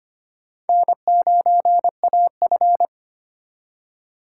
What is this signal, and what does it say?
Morse code, heard as a keyed tone, about 31 wpm, 700 Hz. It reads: N9AF